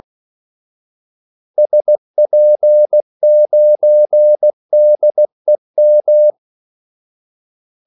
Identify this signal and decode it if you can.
Morse code, heard as a keyed tone, about 16 wpm, 600 Hz. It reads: SP9DEM
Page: Kaldesignaler 16 wpm